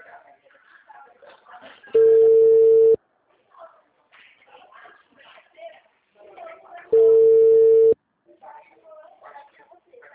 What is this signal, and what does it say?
Ringback tone (China, 450 Hz), with speech.
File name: out- -2009- - - .89